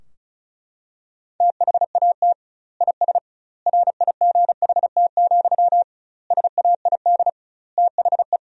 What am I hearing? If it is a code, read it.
Morse code, heard as a keyed tone, about 36 wpm, 700 Hz. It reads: THAT IS RIGHT, SAID THE
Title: 2x Word-Spacing - Sentences from Top 200 Words - 35wpm